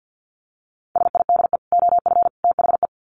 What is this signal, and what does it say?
Morse code, heard as a keyed tone, about 50 wpm, 700 Hz. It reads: SIDE OF THE
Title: Sets of 3 Words 50wpm